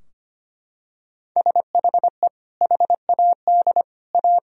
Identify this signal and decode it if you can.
Morse code, heard as a keyed tone, about 25 wpm, 700 Hz. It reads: SHE HAD A
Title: Sentences from Top 1000 Words - Review Part 4 - 25wpm